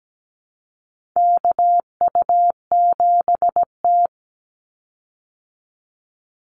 Morse code, heard as a keyed tone, about 17 wpm, 700 Hz. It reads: KU7T
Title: Call Signs Repeated 3x - 17wpm